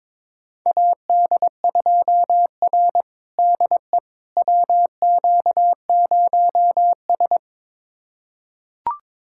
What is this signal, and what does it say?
Morse code, heard as a keyed tone, about 22 wpm, 700 Hz. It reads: AD2R DE WQ0S
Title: USA Callsign DE Callsign - 4 Characters - 22wpm